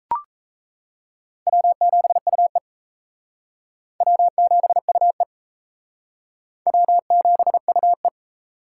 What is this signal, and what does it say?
Morse code, heard as a keyed tone, about 36 wpm, 700 Hz. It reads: W7UE W7UE W7UE